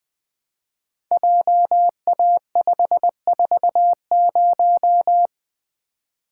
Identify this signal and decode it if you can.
Morse code, heard as a keyed tone, about 20 wpm, 700 Hz. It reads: JA540
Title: Code Groups - 5 Letter:Numbers - 20wpm